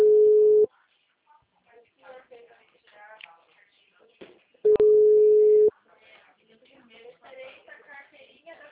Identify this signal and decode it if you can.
Ringback tone (Europe, 425 Hz), with speech.